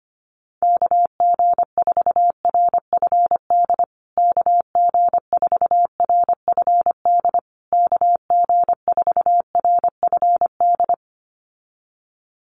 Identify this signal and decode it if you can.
Morse code, heard as a keyed tone, about 25 wpm, 700 Hz. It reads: KG4RFD KG4RFD KG4RFD
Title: USA Call Signs Repeated 3x 25wpm